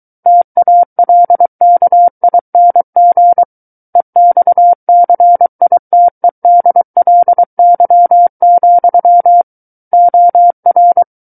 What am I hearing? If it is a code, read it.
Morse code, heard as a keyed tone, about 23 wpm, 700 Hz. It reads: TALKING EXCITEDLY, OR
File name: Book One / 14-Chapter 14 - 23 WPM